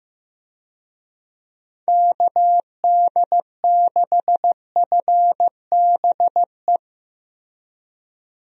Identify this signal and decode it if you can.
Morse code, heard as a keyed tone, about 15 wpm, 700 Hz. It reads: KD6FBE
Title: USA Call Signs - 6 Characters - Repeated 2x - 15wpm